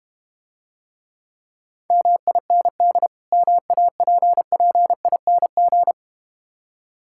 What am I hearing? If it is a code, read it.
Morse code, heard as a keyed tone, about 32 wpm, 700 Hz. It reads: MIND MAPPING